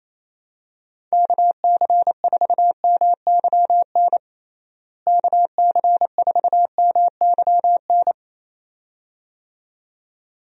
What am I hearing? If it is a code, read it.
Morse code, heard as a keyed tone, about 28 wpm, 700 Hz. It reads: KC4MYN KC4MYN
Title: USA Call Signs - 6 Characters - Repeated 2x - 28wpm